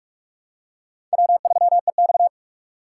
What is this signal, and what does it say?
Morse code, heard as a keyed tone, about 47 wpm, 700 Hz. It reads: W3EX
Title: USA Call Signs - 4 Characters - 2x Character Spacing - 45wpm